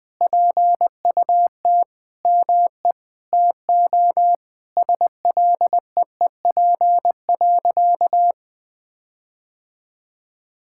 Morse code, heard as a keyed tone, about 20 wpm, 700 Hz. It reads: PUT ME TO SLEEP.
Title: Sentences from Top 1600 Words 20wpm